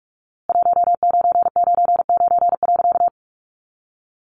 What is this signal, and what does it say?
Morse code, heard as a keyed tone, about 44 wpm, 700 Hz. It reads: 1999.
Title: A Sound Of Thunder 45wpm